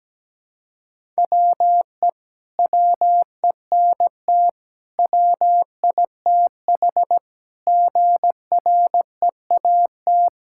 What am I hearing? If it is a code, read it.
Morse code, heard as a keyed tone, about 17 wpm, 700 Hz. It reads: WE WENT WITH GREAT
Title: Sentences from Top 1000 Words - Review Part 5 - 17wpm